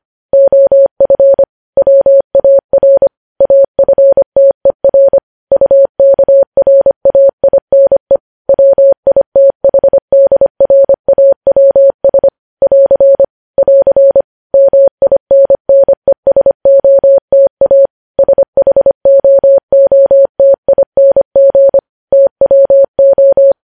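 Morse code, heard as a keyed tone, about 25 wpm, 550 Hz. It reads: OF WAR AFTER UKRAINE WITHDRAWS <AR> <AR> MINNESOTA SHOOTING TWO
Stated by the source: Morse code news headlines sent at 25 wpm.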